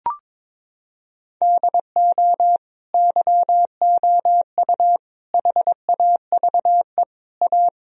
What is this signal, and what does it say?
Morse code, heard as a keyed tone, about 22 wpm, 700 Hz. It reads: DO YOU HAVE A
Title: Sentences from Top 1000 Words - Review Part 3 - 22wpm